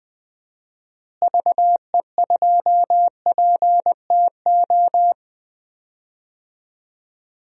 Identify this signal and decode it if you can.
Morse code, heard as a keyed tone, about 20 wpm, 700 Hz. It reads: VE2PTO
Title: DXCC-100 Call Signs - 6 Characters - Repeated 3x - 20wpm